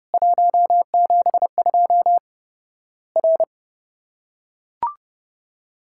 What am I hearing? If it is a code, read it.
Morse code, heard as a keyed tone, about 30 wpm, 700 Hz. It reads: 172 R
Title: Worked All Europe DX Contest - QTC Exchange - 30wpm